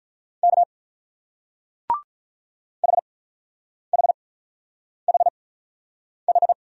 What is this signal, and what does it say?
Morse code, heard as a keyed tone, about 42 wpm, spaced as for 9 wpm, 700 Hz. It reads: K HHHH
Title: Speed-Racing Single Letters 35wpm